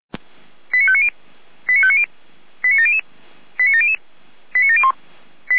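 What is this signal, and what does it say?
Five-tone selective call (ZVEI), heard as quick runs of five tones, tones of about 70 ms. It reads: 89590 89590 89700 89700 89701
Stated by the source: ZVEI five-tone selective call received by SDR.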